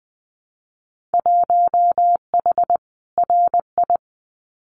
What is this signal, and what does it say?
Morse code, heard as a keyed tone, about 20 wpm, 700 Hz. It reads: 1H RI
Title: Winter Field Day Exchanges 20wpm